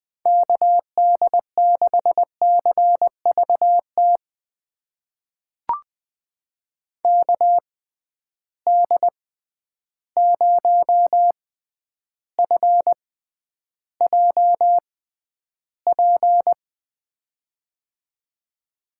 Morse code, heard as a keyed tone, about 20 wpm, spaced as for 10 wpm, 700 Hz. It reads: KD6CVT KD0FJP
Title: USA Call Signs - 6 Characters - 6x Character Spacing - 20wpm